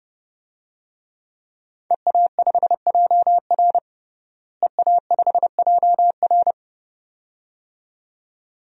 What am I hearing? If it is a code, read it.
Morse code, heard as a keyed tone, about 30 wpm, 700 Hz. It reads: EA5JR EA5JR
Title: DXCC-100 Call Signs - 5 Characters - Repeated 2x - 30wpm